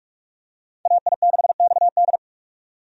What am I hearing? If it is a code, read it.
Morse code, heard as a keyed tone, about 46 wpm, 700 Hz. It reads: AI6XD
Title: USA Call Signs - 5 Characters - 45wpm